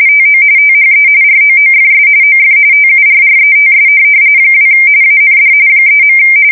RTTY, about 45.45 baud. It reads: TO THE ROOFS OF TURRETS 2 AND 3.